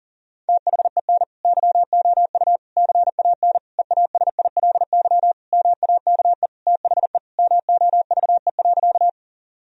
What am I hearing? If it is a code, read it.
Morse code, heard as a keyed tone, about 40 wpm, 700 Hz. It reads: THEN YOU CAN EASILY MAKE THE MOVE.
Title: Sentences from Top 1000 Words - Review Part 1 - 40wpm